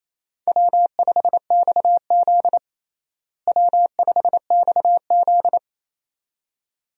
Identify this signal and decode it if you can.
Morse code, heard as a keyed tone, about 28 wpm, 700 Hz. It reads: W5XZ W5XZ
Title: USA Call Signs - 4 Characters - Repeated 2x - 28wpm